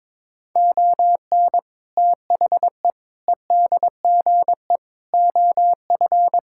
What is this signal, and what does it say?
Morse code, heard as a keyed tone, about 22 wpm, 700 Hz. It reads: ON THE EDGE OF
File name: practice / 22wpm / Sentences with 4-Word Phrases - 22wpm